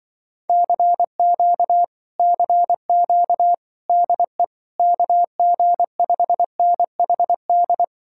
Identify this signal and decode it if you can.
Morse code, heard as a keyed tone, about 24 wpm, 700 Hz. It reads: CQ CQ DE KG5NHD
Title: Speed-Racing Common QSO Phrases Encore - 20wpm